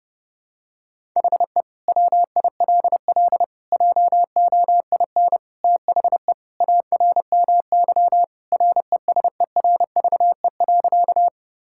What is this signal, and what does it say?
Morse code, heard as a keyed tone, about 32 wpm, 700 Hz. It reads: HE WILL JOIN THE ARMY RESERVE.
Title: Sentences from Top 1000 Words - Review Part 6 - 30wpm